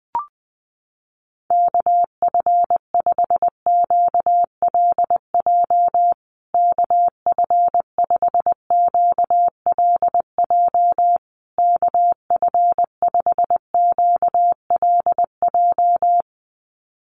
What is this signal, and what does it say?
Morse code, heard as a keyed tone, about 20 wpm, 700 Hz. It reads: KF5QLJ KF5QLJ KF5QLJ
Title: USA Call Signs Repeated 3x 20wpm